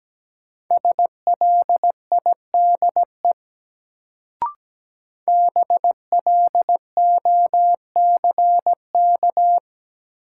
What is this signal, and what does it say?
Morse code, heard as a keyed tone, about 17 wpm, 700 Hz. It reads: SLIDE BLOCK
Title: Sentences from Top 1600 Words - New Word Drill Alt - 17wpm